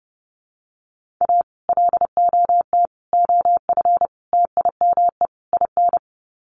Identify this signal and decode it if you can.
Morse code, heard as a keyed tone, about 30 wpm, 700 Hz. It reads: A LOT OF TIME IN
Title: Sets of 5 Words Encore2 30wpm